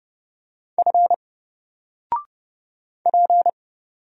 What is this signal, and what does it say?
Morse code, heard as a keyed tone, about 30 wpm, 700 Hz. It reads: F P
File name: Single-Letter-Number-30wpm